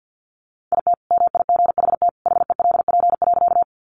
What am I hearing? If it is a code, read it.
Morse code, heard as a keyed tone, about 50 wpm, 700 Hz. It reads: IT MIGHT HELP.
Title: Sentences from Top 300 Words 50wpm